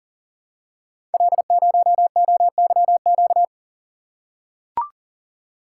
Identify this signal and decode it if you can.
Morse code, heard as a keyed tone, about 40 wpm, 700 Hz. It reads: L0OYQ
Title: Code Groups - 5 Letter:Numbers - 40wpm